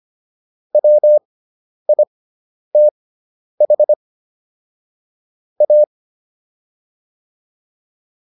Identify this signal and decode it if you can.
Morse code, heard as a keyed tone, about 25 wpm, spaced as for 10 wpm, 600 Hz. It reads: WITH A